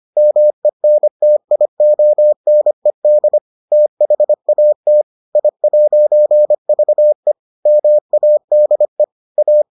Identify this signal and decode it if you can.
Morse code, heard as a keyed tone, about 25 wpm, 600 Hz. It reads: MENTIONED THAT I'VE MADE A